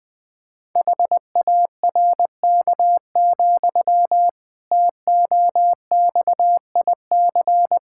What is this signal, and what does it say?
Morse code, heard as a keyed tone, about 20 wpm, 700 Hz. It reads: HARK, TOXIC
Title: Pangram Challenge v2 - 20wpm